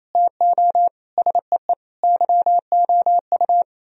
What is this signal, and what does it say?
Morse code, heard as a keyed tone, about 27 wpm, 700 Hz. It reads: TO SEE YOU
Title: Speed-Racing - Sets of 4 Words 28wpm